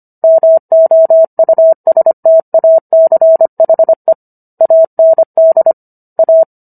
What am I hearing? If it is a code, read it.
Morse code, heard as a keyed tone, about 25 wpm, 650 Hz. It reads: MOUSTACHE AND A